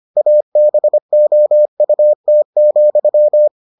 Morse code, heard as a keyed tone, about 25 wpm, 600 Hz. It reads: ABOUT,